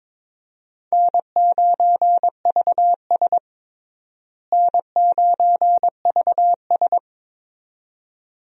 Morse code, heard as a keyed tone, about 22 wpm, 700 Hz. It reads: N9VS N9VS
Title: USA Call Signs - 4 Characters - Repeated 2x - 22wpm